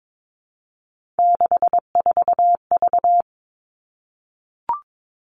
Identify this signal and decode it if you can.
Morse code, heard as a keyed tone, about 22 wpm, 700 Hz. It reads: 64V
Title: All Cut Numbers - 3-Digit 22wpm